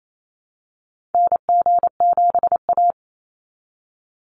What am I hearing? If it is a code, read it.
Morse code, heard as a keyed tone, about 28 wpm, 700 Hz. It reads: NG7A
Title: Call Signs Repeated 3x - 28wpm